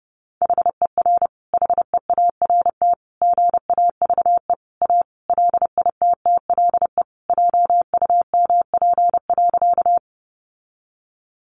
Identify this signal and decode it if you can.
Morse code, heard as a keyed tone, about 30 wpm, 700 Hz. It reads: HER HEART GAVE A LITTLE JUMP.
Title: Sentences from Top 1300 Words 30wpm